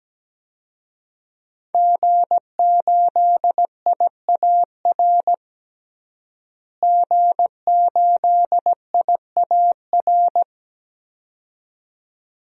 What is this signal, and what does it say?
Morse code, heard as a keyed tone, about 17 wpm, 700 Hz. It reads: G8IAR G8IAR
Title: DXCC-100 Call Signs - 5 Characters - Repeated 2x - 17wpm